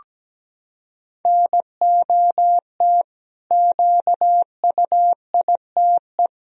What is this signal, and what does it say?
Morse code, heard as a keyed tone, about 17 wpm, 700 Hz. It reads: NOT QUITE
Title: Sentences from Top 1000 Words - Review Part 1 - 17wpm